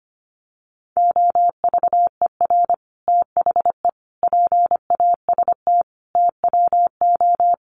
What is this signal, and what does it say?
Morse code, heard as a keyed tone, about 25 wpm, 700 Hz. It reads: OVER THE PAST TWO
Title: Sets of 5 Words Encore2 25wpm